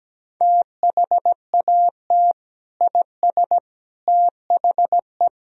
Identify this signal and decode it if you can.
Morse code, heard as a keyed tone, about 17 wpm, 700 Hz. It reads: THAT IS THE
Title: Sentences from Top 1000 Words - Review Part 2 - 17wpm